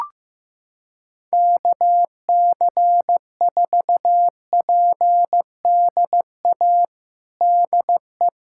Morse code, heard as a keyed tone, about 15 wpm, 700 Hz. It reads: KC4PDA DE
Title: USA Callsign DE Callsign - 6 Characters - 15wpm